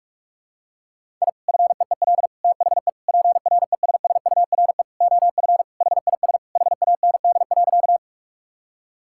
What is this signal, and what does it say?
Morse code, heard as a keyed tone, about 46 wpm, 700 Hz. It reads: I FEEL THE PRESSURE OF HIS HAND.
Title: Sentences from Top 1000 Words - Review Part 6 - 45wpm